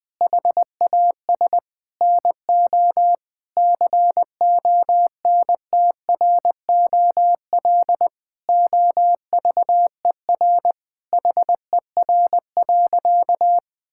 Morse code, heard as a keyed tone, about 21 wpm, 700 Hz. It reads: HAS NO CONTROL OVER HER.
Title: Sentences from Top 1000 Words - Review Part 5 - 20wpm